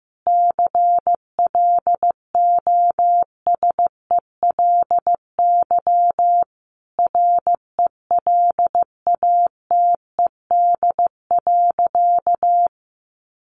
Morse code, heard as a keyed tone, about 15 wpm, 700 Hz. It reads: CLOSELY RELATED.
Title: Sentences from Top 1400 Words 15wpm